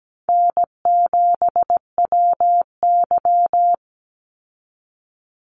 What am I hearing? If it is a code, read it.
Morse code, heard as a keyed tone, about 17 wpm, 700 Hz. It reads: N7WY
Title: Easy Call Signs Repeated 3x - 17wpm